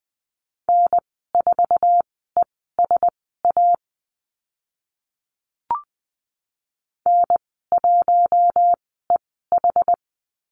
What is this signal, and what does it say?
Morse code, heard as a keyed tone, about 20 wpm, spaced as for 14 wpm, 700 Hz. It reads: N4ESA N1EH
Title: 076 Lesson 13g - Letter L - Call Signs 20wpm